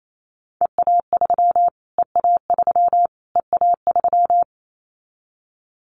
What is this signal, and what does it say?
Morse code, heard as a keyed tone, about 28 wpm, 700 Hz. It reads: EA3 EA3 EA3
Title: Callsign Prefixes for Top 25 DXCC - Repeated 3x - 28wpm